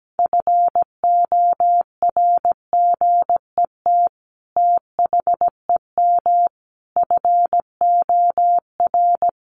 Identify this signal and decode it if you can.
Morse code, heard as a keyed tone, about 17 wpm, 700 Hz. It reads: FORGET THEM FOR
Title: Sentences from Top 1200 Words 17wpm